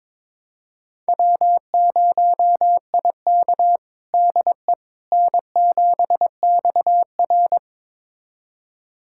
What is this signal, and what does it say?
Morse code, heard as a keyed tone, about 22 wpm, 700 Hz. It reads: W0IK DE N7XR
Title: USA Callsign DE Callsign - 4 Characters - 22wpm